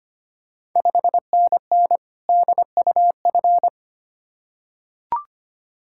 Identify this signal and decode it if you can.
Morse code, heard as a keyed tone, about 25 wpm, 690 Hz. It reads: 5NN DUF
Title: Ontario QSO Party Exchange 25wpm